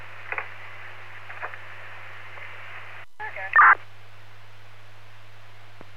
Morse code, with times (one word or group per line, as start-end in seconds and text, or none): none